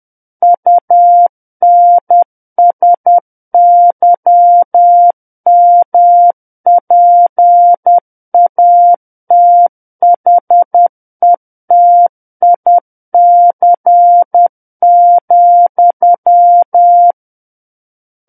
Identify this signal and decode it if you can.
Morse code, heard as a keyed tone, about 10 wpm, 700 Hz. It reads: UNSYMPATHETIC,
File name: Book One / 01-Chapter 1 - 10 WPM